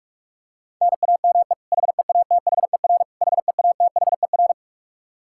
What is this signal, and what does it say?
Morse code, heard as a keyed tone, about 43 wpm, 700 Hz. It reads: NAME HEATHER HEATHER
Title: K1USN SST QSO Practice 45wpm